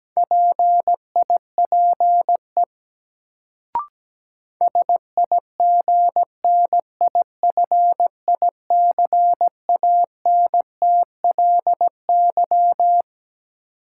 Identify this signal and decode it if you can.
Morse code, heard as a keyed tone, about 17 wpm, 700 Hz. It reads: PIPE SIGNIFICANTLY
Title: Sentences from Top 1500 Words - New Word Drill - 17wpm